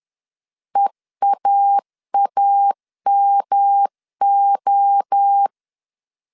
Morse code, heard as a keyed tone, about 10 wpm, 800 Hz. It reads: EAAMO
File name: CW / DET / 1 3 1500C S B 10wpm 3335